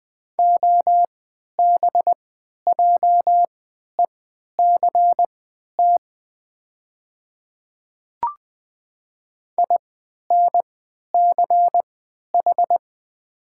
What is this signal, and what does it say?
Morse code, heard as a keyed tone, about 20 wpm, spaced as for 11 wpm, 700 Hz. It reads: OBJECT INCH
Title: 3x Character Spacing - Top 500 Words - 20wpm